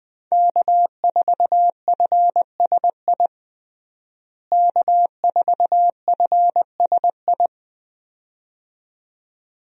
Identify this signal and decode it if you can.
Morse code, heard as a keyed tone, about 21 wpm, 700 Hz. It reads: K4FSI K4FSI
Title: USA Call Signs - 5 Characters - Repeated 2x - 20wpm